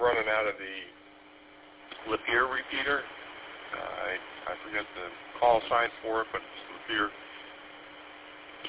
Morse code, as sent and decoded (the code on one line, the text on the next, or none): none